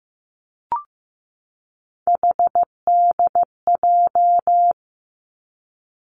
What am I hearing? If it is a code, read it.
Morse code, heard as a keyed tone, about 15 wpm, 700 Hz. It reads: HDJ
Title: Three Challenging Letter-Number Combinations 15wpm